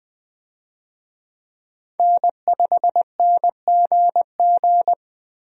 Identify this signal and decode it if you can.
Morse code, heard as a keyed tone, about 20 wpm, 700 Hz. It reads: N5NGG
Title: USA Call Signs - 5 Characters - Repeated 3x - 20wpm